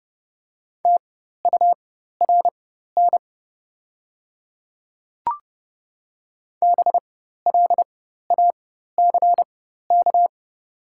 Morse code, heard as a keyed tone, about 30 wpm, spaced as for 14 wpm, 700 Hz. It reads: TURN BLACK
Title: 4x Character Spacing - Top 500 Words - 30wpm